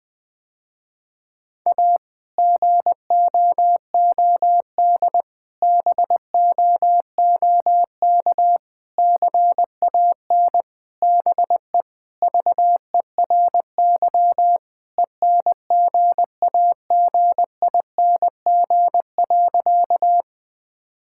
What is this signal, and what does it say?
Morse code, heard as a keyed tone, about 20 wpm, 700 Hz. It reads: A GOOD BOOK CAN BE VERY ENGAGING.
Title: Sentences with 2-Word Phrases - Spoken Beginning - 20wpm